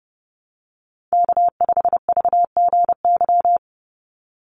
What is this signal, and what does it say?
Morse code, heard as a keyed tone, about 30 wpm, 700 Hz. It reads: K5VGY
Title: USA Call Signs 30wpm